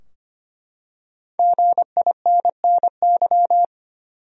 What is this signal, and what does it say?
Morse code, heard as a keyed tone, about 25 wpm, 700 Hz. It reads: GINNY
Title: USA Common Names - 25wpm